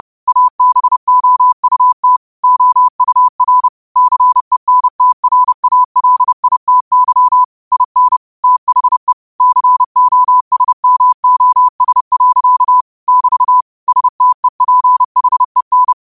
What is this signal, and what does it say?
Morse code, heard as a keyed tone, about 30 wpm, 1000 Hz. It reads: ABOUT OUR CENTRALITY IN THE COSMOS. <BT> STEPHEN
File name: qotdfiles / 30wpm